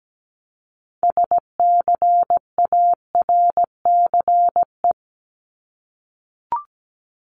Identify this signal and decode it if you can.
Morse code, heard as a keyed tone, about 17 wpm, 700 Hz. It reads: SCARCE
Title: LY Suffix - Top 100 Words with Base Words 17wpm